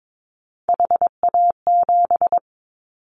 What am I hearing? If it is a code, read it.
Morse code, heard as a keyed tone, about 22 wpm, 700 Hz. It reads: HA7
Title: Callsign Prefixes for Top 25 DXCC - 22wpm